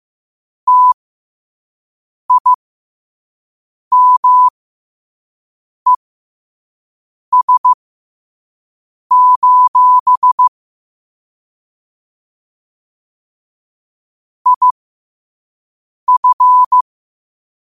Morse code, heard as a keyed tone, about 15 wpm, spaced as for 5 wpm, 1000 Hz. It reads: TIMES: IF
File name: qotdfiles / 5wpm